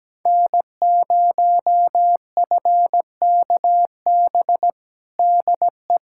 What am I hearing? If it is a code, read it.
Morse code, heard as a keyed tone, about 17 wpm, 700 Hz. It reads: N0FKB DE
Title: USA Callsign DE Callsign - 5 Characters - 17wpm